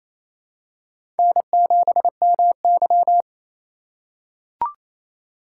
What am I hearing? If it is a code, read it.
Morse code, heard as a keyed tone, about 29 wpm, 700 Hz. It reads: N7MY
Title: USA Call Signs - 4 Characters - 2x Character Spacing - 28wpm